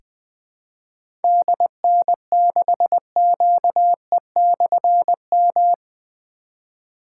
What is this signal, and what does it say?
Morse code, heard as a keyed tone, about 20 wpm, 700 Hz. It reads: DN6QE/M
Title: Difficult Call Signs 20wpm